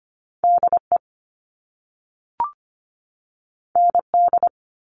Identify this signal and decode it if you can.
Morse code, heard as a keyed tone, about 25 wpm, 700 Hz. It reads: D E N D
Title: All Cut Numbers - 2-Digit 25wpm